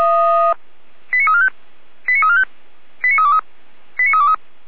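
Five-tone selective call (ZVEI), heard as quick runs of five tones, tones of about 80 ms. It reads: 89345 89345 89323 89323
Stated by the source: ZVEI five-tone selective call received on an SDR.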